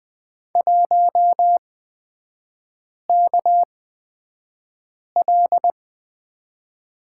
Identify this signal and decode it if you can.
Morse code, heard as a keyed tone, about 20 wpm, spaced as for 5 wpm, 700 Hz. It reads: 1KL
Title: 243 Lesson 39e - Letter z - Three Characters 20wpm